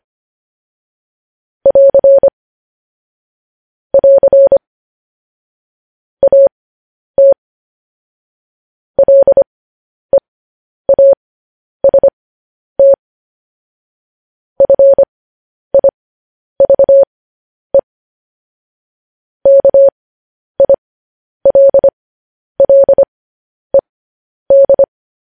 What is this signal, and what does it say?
Morse code, heard as a keyed tone, about 25 wpm, spaced as for 10 wpm, 550 Hz. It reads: <AR> <AR> AT LEAST FIVE KILLED